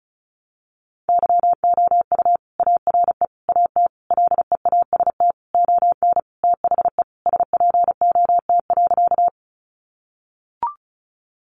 Morse code, heard as a keyed tone, about 34 wpm, 700 Hz. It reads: YOU ARE AT LEAST ON THE SPOT.
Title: Sentences from Top 1100 Words 35wpm